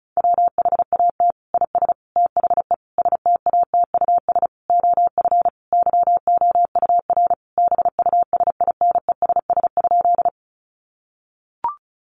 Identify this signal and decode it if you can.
Morse code, heard as a keyed tone, about 34 wpm, 700 Hz. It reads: WHAT IS THE STATUS OF YOUR BUSINESS?
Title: Sentences from Top 1400 Words 35wpm